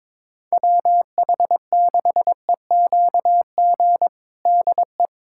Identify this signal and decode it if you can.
Morse code, heard as a keyed tone, about 22 wpm, 700 Hz. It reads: WH6EQG DE
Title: USA Callsign DE Callsign - 6 Characters - 22wpm